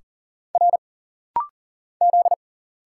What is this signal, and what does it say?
Morse code, heard as a keyed tone, about 40 wpm, 700 Hz. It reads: R Z
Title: Single Letters - Rapid-Fire 40wpm